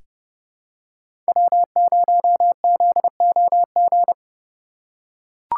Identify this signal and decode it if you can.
Morse code, heard as a keyed tone, about 32 wpm, 700 Hz. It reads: W0ZOG
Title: USA Call Signs - 5 Characters - 2x Character Spacing - 30wpm